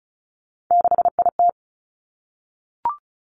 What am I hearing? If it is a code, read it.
Morse code, heard as a keyed tone, about 36 wpm, 700 Hz. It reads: 6IT
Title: Three Challenging Letter-Number Combinations 35wpm